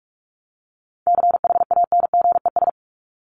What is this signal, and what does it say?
Morse code, heard as a keyed tone, about 44 wpm, 700 Hz. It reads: CHANGES
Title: ES:S Suffix - Top 100 Words with Base Words 45wpm